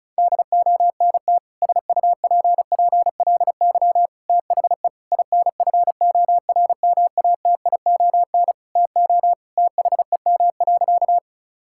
Morse code, heard as a keyed tone, about 36 wpm, 700 Hz. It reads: DONT SUPPLY THE INFORMATION TO THEM.
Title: Sentences from Top 1000 Words - Review Part 5 - 35wpm